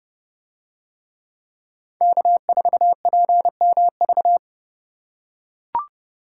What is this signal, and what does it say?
Morse code, heard as a keyed tone, about 30 wpm, 700 Hz. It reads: K4PMV